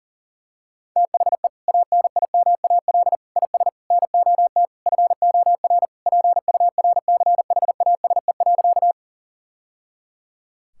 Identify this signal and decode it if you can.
Morse code, heard as a keyed tone, about 37 wpm, 700 Hz. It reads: THE ANIMAL IS NOT FOR PURCHASE.
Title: Sentences from Top 1000 Words - Review Part 5 - 40wpm